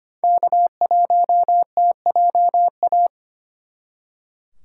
Morse code, heard as a keyed tone, about 34 wpm, 700 Hz. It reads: K1TJA